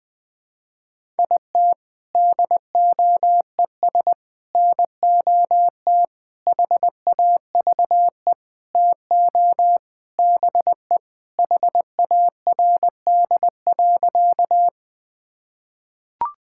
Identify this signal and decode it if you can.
Morse code, heard as a keyed tone, about 20 wpm, 700 Hz. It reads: IT DOES NOT HAVE TO BE HARD.
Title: Sentences with 4-Word Phrases - 20wpm